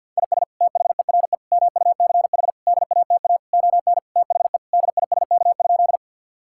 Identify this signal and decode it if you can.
Morse code, heard as a keyed tone, about 50 wpm, 700 Hz. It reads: IS THERE MUCH DATA ON THE DISK?
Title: Sentences from Top 1000 Words - Review Part 6 - 50wpm